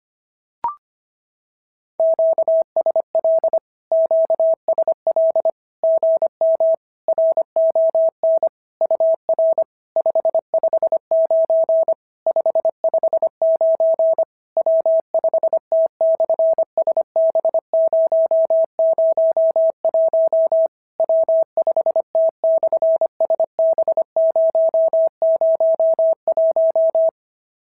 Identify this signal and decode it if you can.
Morse code, heard as a keyed tone, about 25 wpm, 650 Hz. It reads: QSL QSL GM RON UR 559 559 W5T/SB001 W5T/SB001
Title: SOTA QSO Practice 25wpm